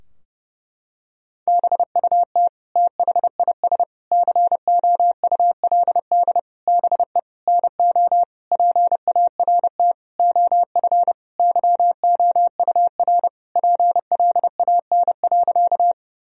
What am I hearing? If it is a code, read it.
Morse code, heard as a keyed tone, about 30 wpm, 700 Hz. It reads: BUT THIS COULD BE NO PART OF YOUR PLAN.
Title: Sentences from Top 1000 Words - Review Part 2 - 30wpm